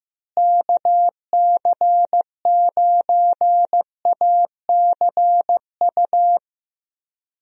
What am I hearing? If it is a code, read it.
Morse code, heard as a keyed tone, about 15 wpm, 700 Hz. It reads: KC9ACU
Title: USA Call Signs - 6 Characters - Repeated 3x - 15wpm